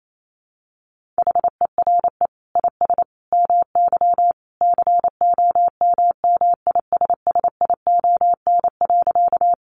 Morse code, heard as a keyed tone, about 28 wpm, 700 Hz. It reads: HERE IS MY COMMISSION.
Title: Sentences from Top 1400 Words 28wpm